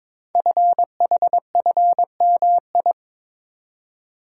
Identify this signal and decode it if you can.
Morse code, heard as a keyed tone, about 22 wpm, 700 Hz. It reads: FHFMI
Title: Code Groups - 5 Letter:Numbers - 22wpm